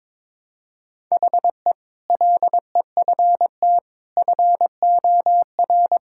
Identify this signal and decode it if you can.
Morse code, heard as a keyed tone, about 22 wpm, 700 Hz. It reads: HE LEFT FOR
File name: practice / 22wpm / Sentences with 8-Word Phrases - 22wpm